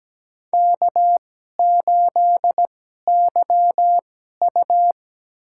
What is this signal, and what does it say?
Morse code, heard as a keyed tone, about 17 wpm, spaced as for 12 wpm, 700 Hz. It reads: K8YU
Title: USA Call Signs - 4 Characters - 2x Character Spacing - 17wpm